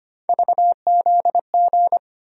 Morse code, heard as a keyed tone, about 25 wpm, 700 Hz. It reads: VZG
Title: USA Call Sign Postfixes - 3 Letters 25wpm